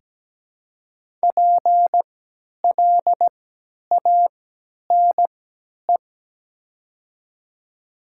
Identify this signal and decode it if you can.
Morse code, heard as a keyed tone, about 17 wpm, spaced as for 10 wpm, 700 Hz. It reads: PLANE
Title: 3x Character Spacing - Top 500 Words - 17wpm